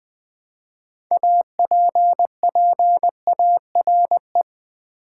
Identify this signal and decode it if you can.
Morse code, heard as a keyed tone, about 20 wpm, 700 Hz. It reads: APPARE
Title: Build A Word - 3 Letters At A Time - 8-Letter Words - 20wpm